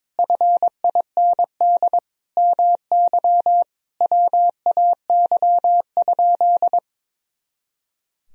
Morse code, heard as a keyed tone, about 22 wpm, 700 Hz. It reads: FIND MY WAY?
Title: Sentences from Top 200 Words and Abbreviations 22wpm